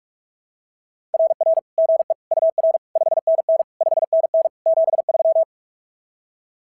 Morse code, heard as a keyed tone, about 44 wpm, 660 Hz. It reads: RR GE UR 5NN 5NN 73
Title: SOTA QSO Practice 45wpm